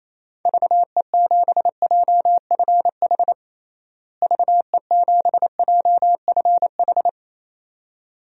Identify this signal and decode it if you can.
Morse code, heard as a keyed tone, about 28 wpm, 700 Hz. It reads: VE7JFH VE7JFH
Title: DXCC-100 Call Signs - 6 Characters - Repeated 2x - 28wpm